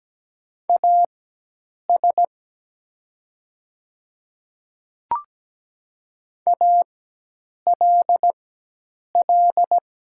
Morse code, heard as a keyed tone, about 17 wpm, spaced as for 8 wpm, 700 Hz. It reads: AS ALL
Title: 4x Character Spacing - Top 100 Words - 17wpm